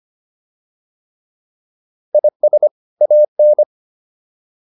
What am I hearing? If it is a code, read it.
Morse code, heard as a keyed tone, about 25 wpm, 600 Hz. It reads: IS AN